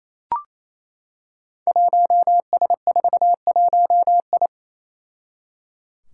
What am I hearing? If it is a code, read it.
Morse code, heard as a keyed tone, about 28 wpm, 700 Hz. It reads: 1S41I
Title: Five Challenging Letter-Number Combinations 28wpm